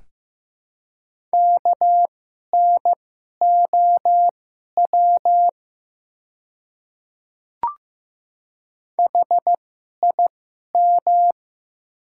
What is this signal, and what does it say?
Morse code, heard as a keyed tone, about 15 wpm, spaced as for 11 wpm, 700 Hz. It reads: KNOW HIM
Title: 2x Character Spacing - Top 300 Words - 15wpm